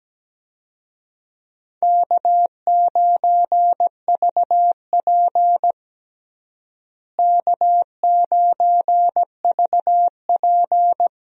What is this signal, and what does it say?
Morse code, heard as a keyed tone, about 17 wpm, 700 Hz. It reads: K9VP K9VP
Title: USA Call Signs - 4 Characters - Repeated 2x - 17wpm